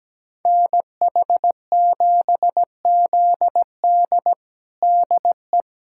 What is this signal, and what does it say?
Morse code, heard as a keyed tone, about 17 wpm, 700 Hz. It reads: NH7ZD DE
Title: USA Callsign DE Callsign - 5 Characters - 17wpm